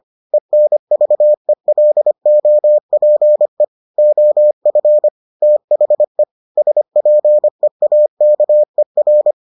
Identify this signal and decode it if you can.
Morse code, heard as a keyed tone, about 25 wpm, 600 Hz. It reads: ENVELOPE OF THE SPEAKER